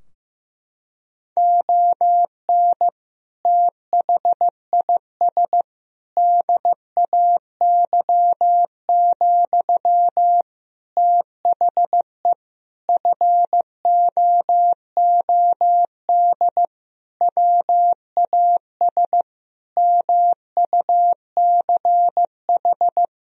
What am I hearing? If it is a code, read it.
Morse code, heard as a keyed tone, about 14 wpm, 700 Hz. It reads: ON THIS DAY, THE FOOD WAS MUCH
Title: Sentences from Top 1000 Words - Review Part 2 - 15wpm